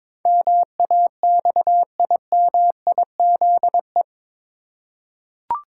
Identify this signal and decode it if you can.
Morse code, heard as a keyed tone, about 22 wpm, 700 Hz. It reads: MAXIMIZE
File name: practice / 22wpm / Eight Letter Words - 22wpm